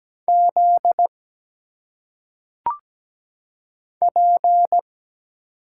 Morse code, heard as a keyed tone, about 17 wpm, 700 Hz. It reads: Z P
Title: Single Letters - Phonetic Pronunciation 17wpm